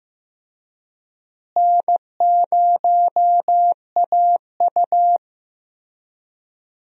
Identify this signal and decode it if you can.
Morse code, heard as a keyed tone, about 15 wpm, 700 Hz. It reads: N0AU
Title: USA Call Signs - 4 Characters - Repeated 3x - 15wpm